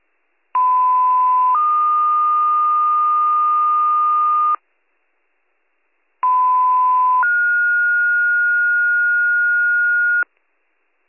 Two-tone page, heard as a one-second tone, then a three-second tone, about 1010 then 1220 Hz, about 1010 then 1430 Hz, with noise. It is heard from 0.5 to 10.2 s.